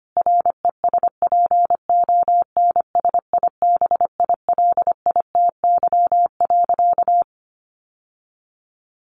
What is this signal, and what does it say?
Morse code, heard as a keyed tone, about 25 wpm, 700 Hz. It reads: RESPONSIBILITY.
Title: Sentences from Top 1200 Words 25wpm